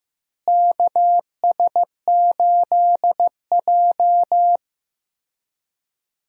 Morse code, heard as a keyed tone, about 15 wpm, 700 Hz. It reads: KS8J
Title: USA Call Signs - 4 Characters - Repeated 3x - 15wpm